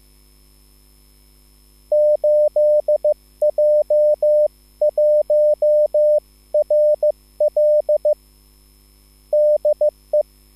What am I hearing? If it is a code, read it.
Morse code, heard as a keyed tone, about 15 wpm, 600 Hz. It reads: 8J1RL DE
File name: QSL / CW lesson